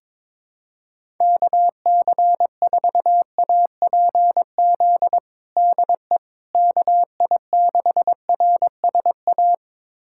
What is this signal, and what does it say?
Morse code, heard as a keyed tone, about 22 wpm, 700 Hz. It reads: KC4APZ DE KI6RSA
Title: USA Callsign DE Callsign - 6 Characters - 22wpm